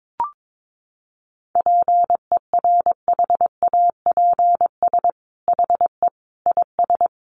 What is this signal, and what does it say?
Morse code, heard as a keyed tone, about 22 wpm, 700 Hz. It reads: PERHAPS HE IS
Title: Sentences from Top 1500 Words 22wpm